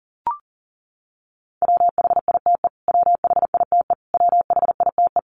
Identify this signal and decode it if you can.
Morse code, heard as a keyed tone, about 37 wpm, 700 Hz. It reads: WHITE WHITE WHITE
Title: Top 500 Words Repeated 40wpm